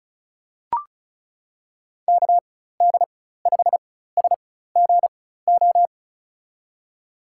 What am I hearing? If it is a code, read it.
Morse code, heard as a keyed tone, about 36 wpm, spaced as for 16 wpm, 700 Hz. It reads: KD5SGO
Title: USA Call Signs - 6 Characters - 4x Character Spacing - 35wpm